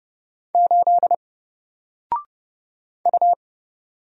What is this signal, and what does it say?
Morse code, heard as a keyed tone, about 30 wpm, 700 Hz. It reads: 8 U
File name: Single-Letter-Number-30wpm